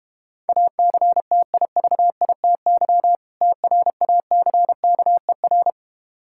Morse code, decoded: ACTIVITY TRACKER

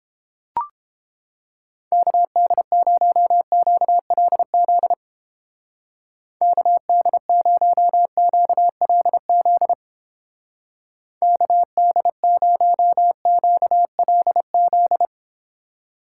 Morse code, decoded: KD0QLZ KD0QLZ KD0QLZ

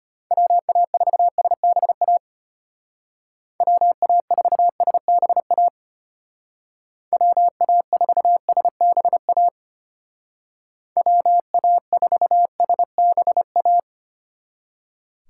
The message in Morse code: WA4SBA WA4SBA WA4SBA WA4SBA

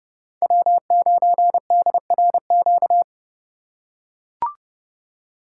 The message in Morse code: W9DRQ

